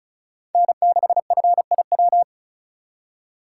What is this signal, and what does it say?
Morse code, heard as a keyed tone, about 46 wpm, 700 Hz. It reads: N6FIW